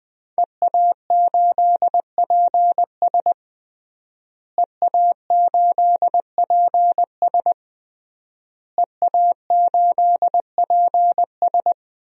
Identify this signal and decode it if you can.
Morse code, heard as a keyed tone, about 21 wpm, 700 Hz. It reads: EA8PS EA8PS EA8PS
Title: DXCC-100 Call Signs - 5 Characters - Repeated 3x - 20wpm